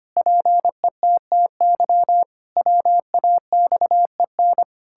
Morse code, heard as a keyed tone, about 25 wpm, 700 Hz. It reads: PETTY WAXEN
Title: Pangram Challenge v2 - 25wpm